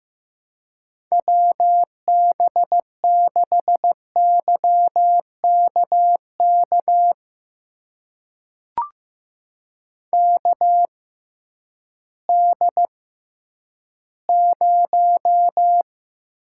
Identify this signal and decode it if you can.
Morse code, heard as a keyed tone, about 15 wpm, spaced as for 10 wpm, 700 Hz. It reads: WB6YKK KD0
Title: USA Call Signs - 6 Characters - 6x Character Spacing - 15wpm